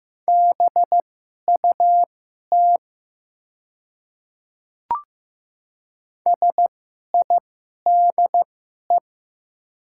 Morse code, heard as a keyed tone, about 15 wpm, spaced as for 11 wpm, 700 Hz. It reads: BUT SIDE
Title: 2x Character Spacing - Top 100 Words - 15wpm